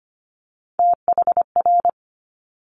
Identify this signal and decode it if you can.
Morse code, heard as a keyed tone, about 25 wpm, 700 Hz. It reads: THR
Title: Multi-letter Phonemes Quick Fire 25wpm